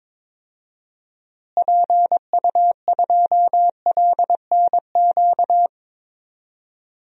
Morse code, heard as a keyed tone, about 21 wpm, 700 Hz. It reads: PU2LNQ
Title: DXCC-100 Call Signs - 6 Characters - Repeated 2x - 22wpm